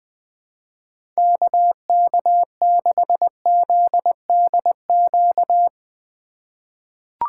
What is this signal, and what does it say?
Morse code, heard as a keyed tone, about 21 wpm, 700 Hz. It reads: KK6ZDQ
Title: USA Call Signs - 6 Characters - 3x Character Spacing - 20wpm